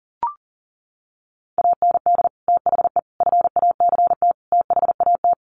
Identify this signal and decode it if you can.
Morse code, heard as a keyed tone, about 40 wpm, 700 Hz. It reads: AND THE FACT THAT
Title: Sets of 4 Words 40wpm